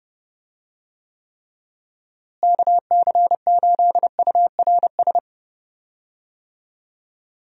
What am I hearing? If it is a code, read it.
Morse code, heard as a keyed tone, about 30 wpm, 700 Hz. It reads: KC8URS
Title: USA Call Signs - 6 Characters - Repeated 3x - 30wpm